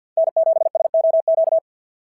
Morse code, heard as a keyed tone, about 51 wpm, 690 Hz. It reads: N7SOQ
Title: SOTA QSO Practice 50wpm